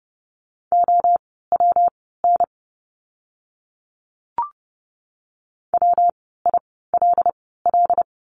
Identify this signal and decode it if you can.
Morse code, heard as a keyed tone, about 30 wpm, spaced as for 17 wpm, 700 Hz. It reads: OWN WILL
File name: course / 30:10wpm / 115 Lesson 19f - Letter w - Words 30wpm